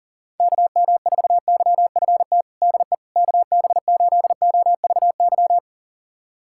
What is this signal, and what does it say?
Morse code, heard as a keyed tone, about 40 wpm, 700 Hz. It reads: KM4YFT DE KB8OVY
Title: USA Callsign DE Callsign - 6 Characters - 40wpm